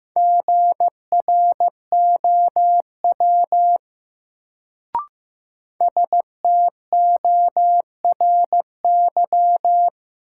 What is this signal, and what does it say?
Morse code, heard as a keyed tone, about 15 wpm, 700 Hz. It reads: GROW STORY
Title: Sentences from Top 300 Words - New Word Drill - 15wpm